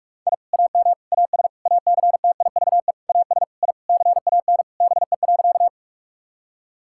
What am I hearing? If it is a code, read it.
Morse code, heard as a keyed tone, about 41 wpm, 700 Hz. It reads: I AM AS ACTIVE AS I CAN BE.
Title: Sentences from Top 1000 Words - Review Part 6 - 45wpm